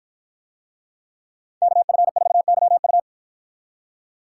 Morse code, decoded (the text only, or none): KF4YU